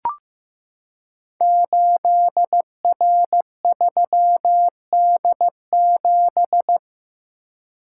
Morse code, heard as a keyed tone, about 15 wpm, 700 Hz. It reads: 8R3D7
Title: Code Groups - 5 Letter:Numbers - 15wpm